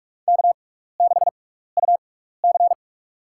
Morse code, heard as a keyed tone, about 46 wpm, spaced as for 15 wpm, 700 Hz. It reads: K6UC
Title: USA Call Signs - 4 Characters - 6x Character Spacing - 45wpm